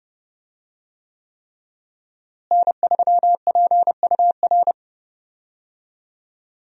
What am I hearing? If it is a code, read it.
Morse code, heard as a keyed tone, about 30 wpm, 700 Hz. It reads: N3PUR
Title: USA Call Signs - 5 Characters - Repeated 3x - 30wpm